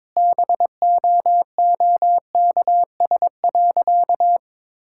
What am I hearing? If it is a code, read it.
Morse code, heard as a keyed tone, about 22 wpm, 700 Hz. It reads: BOOKS.
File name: practice / 22wpm / Sentences with 3-Word Phrases - 22wpm